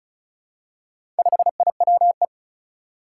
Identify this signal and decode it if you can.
Morse code, heard as a keyed tone, about 36 wpm, 700 Hz. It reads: 5IWE
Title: Four Challenging Letter-Number Combinations 35wpm